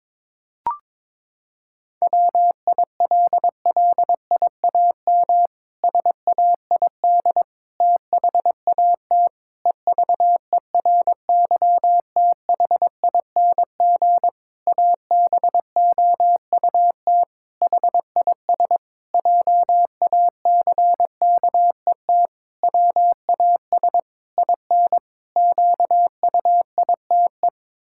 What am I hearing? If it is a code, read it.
Morse code, heard as a keyed tone, about 22 wpm, 700 Hz. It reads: WILLIAM SAID THAT EVERYTHING ABOUT HIS JACKET WAS IN QUITE
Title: Pangram Challenge v2 - 22wpm